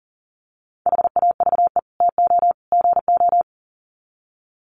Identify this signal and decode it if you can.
Morse code, heard as a keyed tone, about 40 wpm, 700 Hz. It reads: HAVE TO GO
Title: Sets of 3 Words 40wpm